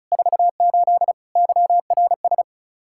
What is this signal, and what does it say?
Morse code, heard as a keyed tone, about 36 wpm, 700 Hz. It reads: 48 YRS
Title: Practice QSO - Exchange 3 - More Info - 35wpm